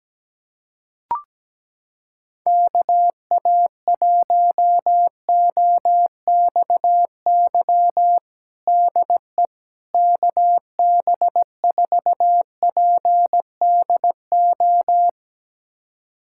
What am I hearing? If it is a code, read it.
Morse code, heard as a keyed tone, about 17 wpm, 700 Hz. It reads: KA1OXY DE KB4PDO
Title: USA Callsign DE Callsign - 6 Characters - 17wpm